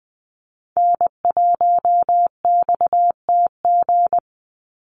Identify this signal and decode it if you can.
Morse code, heard as a keyed tone, about 20 wpm, 700 Hz. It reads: N1XTG
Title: USA Call Signs 20wpm